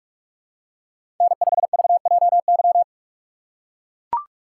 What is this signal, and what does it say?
Morse code, heard as a keyed tone, about 44 wpm, 700 Hz. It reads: N5VJY
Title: USA Call Signs - 6x Character Spacing - 45wpm